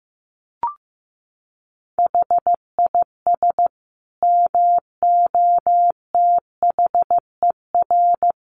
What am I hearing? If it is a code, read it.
Morse code, heard as a keyed tone, about 15 wpm, 700 Hz. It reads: HIS MOTHER
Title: Sentences from Top 800 Words 15wpm